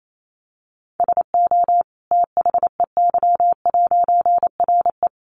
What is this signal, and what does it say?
Morse code, heard as a keyed tone, about 28 wpm, 700 Hz. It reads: SO THEY'RE
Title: A Sound Of Thunder 28wpm